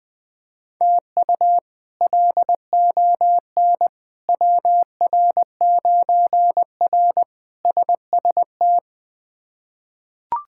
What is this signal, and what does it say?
Morse code, heard as a keyed tone, about 20 wpm, 700 Hz. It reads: TU LON WR9R SST
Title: K1USN SST QSO Practice - 20wpm